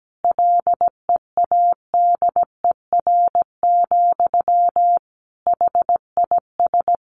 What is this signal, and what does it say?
Morse code, heard as a keyed tone, about 17 wpm, 700 Hz. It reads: LEADER, HIS
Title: A Sound Of Thunder 17wpm